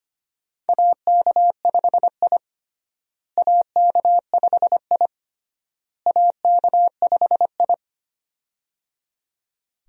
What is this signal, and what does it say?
Morse code, heard as a keyed tone, about 25 wpm, 700 Hz. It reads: AK5I AK5I AK5I
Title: USA Call Signs - 4 Characters - Repeated 3x - 25wpm